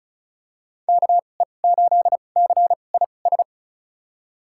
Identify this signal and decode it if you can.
Morse code, heard as a keyed tone, about 34 wpm, spaced as for 25 wpm, 700 Hz. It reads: KE8CIS
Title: USA Call Signs - 6 Characters - 2x Character Spacing - 35wpm